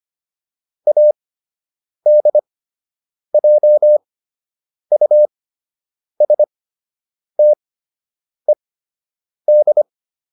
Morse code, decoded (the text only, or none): ADJUSTED